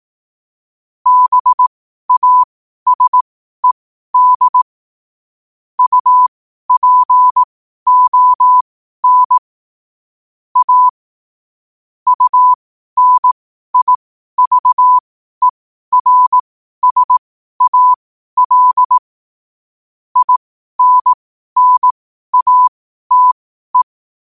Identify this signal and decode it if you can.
Morse code, heard as a keyed tone, about 18 wpm, spaced as for 12 wpm, 1000 Hz. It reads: BASED UPON A UNIVERSAL INNATE
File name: qotdfiles / 12wpm